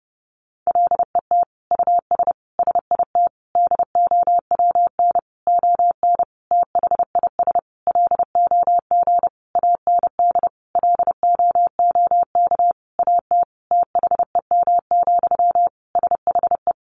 Morse code, heard as a keyed tone, about 30 wpm, 700 Hz. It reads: LET US SIT DOWN ON THIS LOG AND LOOK AT THEM, SHE
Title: Sentences from Top 1400 Words 30wpm